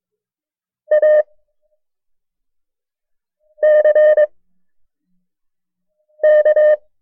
Morse code, heard as a keyed tone, about 22 wpm, 610 Hz. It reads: A C K
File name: FreeFiles / CW-practice / KB6MT-tape1-side2